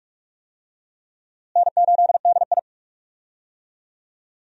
Morse code, heard as a keyed tone, about 41 wpm, 700 Hz. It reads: N8DI
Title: USA Call Signs - 4 Characters - Repeated 3x - 45wpm